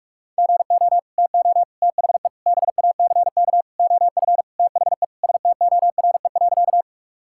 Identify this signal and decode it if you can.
Morse code, heard as a keyed tone, about 44 wpm, 700 Hz. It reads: GO TO THE BACK OF THE STORE.
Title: Sentences from Top 1000 Words - Review Part 5 - 45wpm